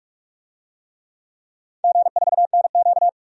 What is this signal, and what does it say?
Morse code, heard as a keyed tone, about 43 wpm, 700 Hz. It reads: G4NQ
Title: DXCC-100 Call Signs - 4 Characters - 45wpm